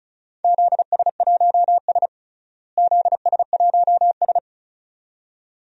Morse code, heard as a keyed tone, about 36 wpm, 700 Hz. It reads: ZS1S ZS1S
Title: DXCC-100 Call Signs - 4 Characters - Repeated 2x - 35wpm